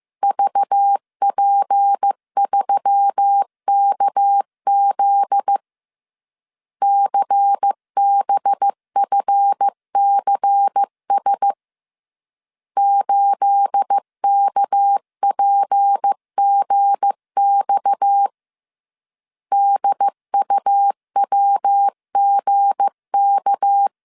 Morse code, heard as a keyed tone, about 15 wpm, 800 Hz. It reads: VP3KZ CBFCS 8KPGX DUWGK
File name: CW / DET / 1 9 1500C S A 14wpm 3222